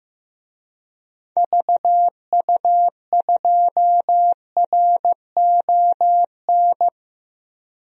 Morse code, heard as a keyed tone, about 15 wpm, 700 Hz. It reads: VU2RON
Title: DXCC-100 Call Signs - 6 Characters - 15wpm